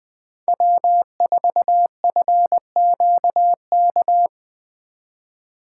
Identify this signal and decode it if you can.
Morse code, heard as a keyed tone, about 20 wpm, 700 Hz. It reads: W4FQK
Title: USA Call Signs - 5 Characters - 3x Character Spacing - 20wpm